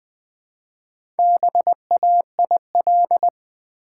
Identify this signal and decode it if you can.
Morse code, heard as a keyed tone, about 20 wpm, 700 Hz. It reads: BAIL
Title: Build A Word - 1 Letter At A Time - 3 to 5-Letter Words - 20wpm